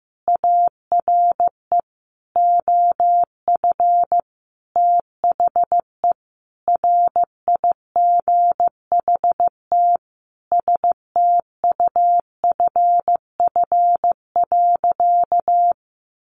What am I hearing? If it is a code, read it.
Morse code, heard as a keyed tone, about 15 wpm, 700 Hz. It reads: ARE OF THE RIGHT STUFF.
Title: Sentences from Top 1400 Words 15wpm